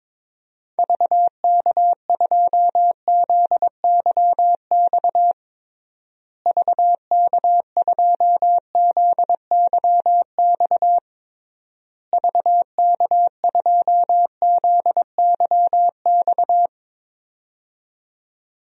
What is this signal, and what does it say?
Morse code, heard as a keyed tone, about 22 wpm, 700 Hz. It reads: VK2ZYX VK2ZYX VK2ZYX
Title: DXCC-100 Call Signs - 6 Characters - Repeated 3x - 22wpm